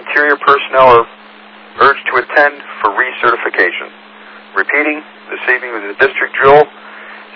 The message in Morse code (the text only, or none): none